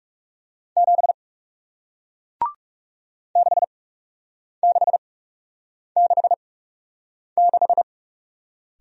Morse code, heard as a keyed tone, about 37 wpm, spaced as for 8 wpm, 700 Hz. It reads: 7 6666
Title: Speed-Racing Single Letter-Number 30wpm